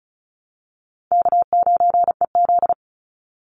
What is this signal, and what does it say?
Morse code, heard as a keyed tone, about 36 wpm, 700 Hz. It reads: K9EZ
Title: NA SOTA Chaser Callsigns 35wpm